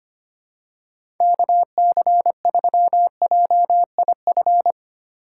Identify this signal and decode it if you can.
Morse code, heard as a keyed tone, about 25 wpm, 700 Hz. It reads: KC3JIF
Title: USA Call Signs - 6 Characters - 3x Character Spacing - 25wpm